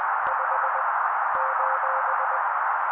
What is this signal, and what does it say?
Morse code, heard as a keyed tone, about 20 wpm, 530 Hz. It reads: 5 :
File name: daily / 18wpm noise